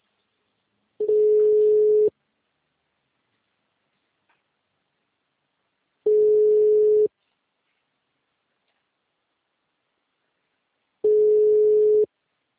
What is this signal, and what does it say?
Ringback tone (Europe, 425 Hz).
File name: out- -2020- - - .119